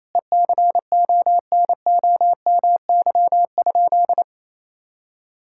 Morse code, heard as a keyed tone, about 27 wpm, 700 Hz. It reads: ECONOMY?
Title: Sentences from Top 1000 Words - Review Part 6 - 28wpm